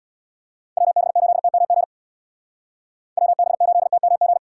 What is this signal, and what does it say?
Morse code, heard as a keyed tone, about 59 wpm, 700 Hz. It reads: WB8TMG WB8TMG
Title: Speed Racing - USA 2x3 Call Signs 50wpm